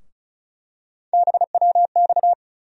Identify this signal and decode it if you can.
Morse code, heard as a keyed tone, about 34 wpm, 700 Hz. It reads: BWX
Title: USA Call Sign Postfixes - 3 Letters 35wpm